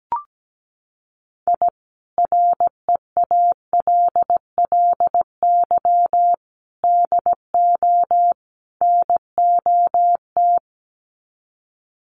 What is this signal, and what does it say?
Morse code, heard as a keyed tone, about 17 wpm, 700 Hz. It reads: I REALLY DO NOT
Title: Sets of 4 Words 17wpm